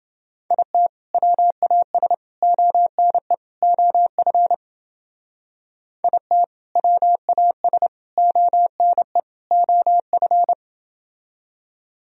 Morse code, heard as a keyed tone, about 31 wpm, 700 Hz. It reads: IT WAS ONE OF IT WAS ONE OF